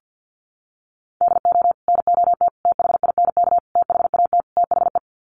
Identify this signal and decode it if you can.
Morse code, heard as a keyed tone, about 50 wpm, 700 Hz. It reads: DO NOT THINK THAT THE
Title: Sets of 5 Words Encore2 50wpm